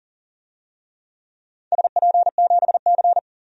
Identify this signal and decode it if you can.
Morse code, heard as a keyed tone, about 40 wpm, 700 Hz. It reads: SP7C
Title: DXCC-100 Call Signs - 4 Characters - 40wpm